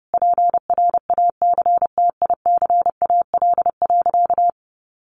Morse code, heard as a keyed tone, about 30 wpm, 700 Hz. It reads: PRACTICAL.
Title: Sentences from Top 1400 Words 30wpm